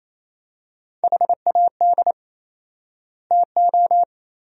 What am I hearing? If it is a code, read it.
Morse code, heard as a keyed tone, about 28 wpm, 700 Hz. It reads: HAD TO
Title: 4x Word-Spacing - Sentences from Top 100 Words - 28wpm